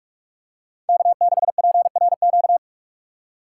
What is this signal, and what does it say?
Morse code, heard as a keyed tone, about 47 wpm, 690 Hz. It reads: K6PRQ
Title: USA Call Signs - 5 Characters - 6x Character Spacing - 45wpm